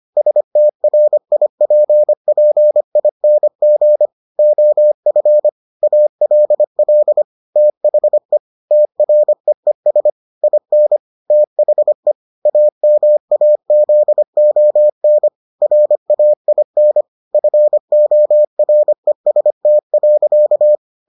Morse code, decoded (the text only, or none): STRIPPING OF ALL THE TREES IN THE AMAZON RAIN FOREST.